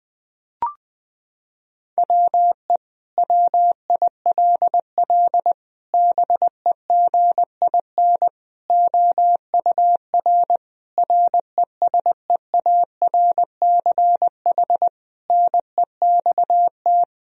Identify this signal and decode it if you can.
Morse code, heard as a keyed tone, about 20 wpm, 700 Hz. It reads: WE WILL BEGIN OUR RESEARCH NEXT
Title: Sentences from Top 1000 Words - Review Part 4 - 20wpm